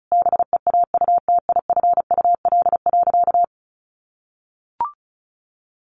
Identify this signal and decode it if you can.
Morse code, heard as a keyed tone, about 34 wpm, 700 Hz. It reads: BEAUTIFUL.
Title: Sentences from Top 900 Words 35wpm